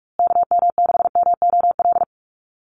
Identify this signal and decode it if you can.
Morse code, heard as a keyed tone, about 44 wpm, 690 Hz. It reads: KM6MOL
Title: USA Call Signs Repeated 2x 45wpm